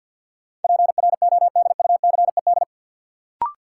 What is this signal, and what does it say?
Morse code, heard as a keyed tone, about 53 wpm, 700 Hz. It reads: PRODUCED
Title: Sentences from Top 1000 Words - Review Part 6 - New Word Drill - 50wpm